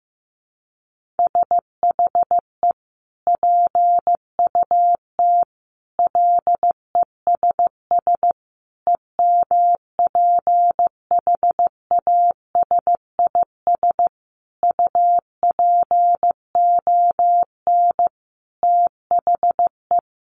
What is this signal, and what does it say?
Morse code, heard as a keyed tone, about 15 wpm, 700 Hz. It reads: SHE PUT LESS EMPHASIS UPON THE
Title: Sentences from Top 1400 Words 15wpm